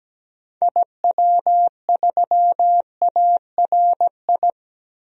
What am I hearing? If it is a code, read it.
Morse code, heard as a keyed tone, about 17 wpm, 700 Hz. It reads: IW3ARI
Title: DXCC-100 Call Signs - 6 Characters - 17wpm